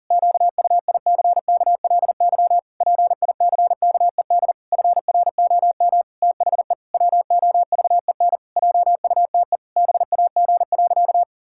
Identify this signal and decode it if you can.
Morse code, heard as a keyed tone, about 40 wpm, 700 Hz. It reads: QUICKLY PICKED FROM THE WOVEN JUTE BAG.
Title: Pangram Challenge v2 - 40wpm